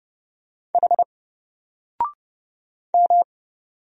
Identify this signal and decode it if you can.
Morse code, heard as a keyed tone, about 31 wpm, 700 Hz. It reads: H M
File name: Single-Letter-Number-30wpm